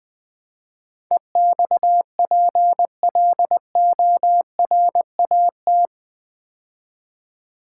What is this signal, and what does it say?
Morse code, heard as a keyed tone, about 20 wpm, 700 Hz. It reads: EXPLORAT
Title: Build A Word - 4 Letters At A Time - 11-Letter Words - 20wpm